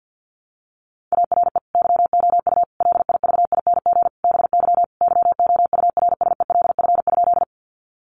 Morse code, heard as a keyed tone, about 50 wpm, 700 Hz. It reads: ARE YOU LIVING BY YOURSELF?
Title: Sentences from Top 1100 Words 50wpm